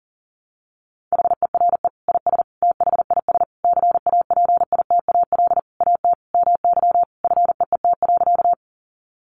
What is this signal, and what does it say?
Morse code, heard as a keyed tone, about 37 wpm, 700 Hz. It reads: HERE IS THIS CAPITAL AT MY FEET.
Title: Sentences from Top 1000 Words 40wpm